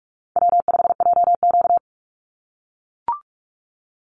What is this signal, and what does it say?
Morse code, heard as a keyed tone, about 44 wpm, 700 Hz. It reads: W5JQ
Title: Easy Call Signs - 45wpm